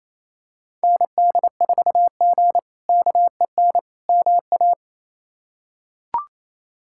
Morse code, heard as a keyed tone, about 27 wpm, 700 Hz. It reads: ND4G KEN MA
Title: K1USN SST QSO Practice - 28wpm